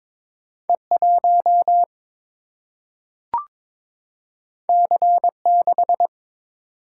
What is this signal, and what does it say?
Morse code, heard as a keyed tone, about 22 wpm, 700 Hz. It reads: E1 C6
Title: Single-Letter Single-Number 22wpm